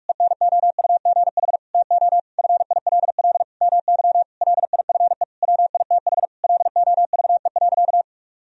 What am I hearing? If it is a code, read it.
Morse code, heard as a keyed tone, about 46 wpm, 700 Hz. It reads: ENOUGH TO FILL MY LIFE WITH LOVE.
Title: Sentences from Top 1000 Words - Review Part 3 - 45wpm